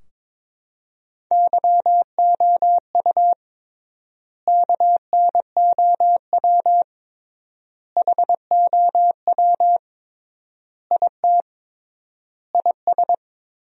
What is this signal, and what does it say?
Morse code, heard as a keyed tone, about 22 wpm, 700 Hz. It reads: YOU KNOW HOW IT IS
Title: 3x Word-Spacing - Sentences from Top 100 Words - 22wpm